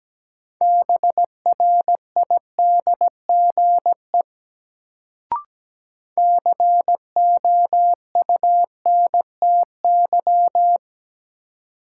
Morse code, heard as a keyed tone, about 17 wpm, 700 Hz. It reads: BRIDGE COUNTY
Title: Sentences from Top 1200 Words - New Word Drill Alt - 17wpm